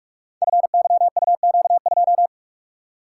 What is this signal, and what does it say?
Morse code, heard as a keyed tone, about 44 wpm, 700 Hz. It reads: FYUQ2
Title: Code Groups - 5 Letter:Numbers - 45wpm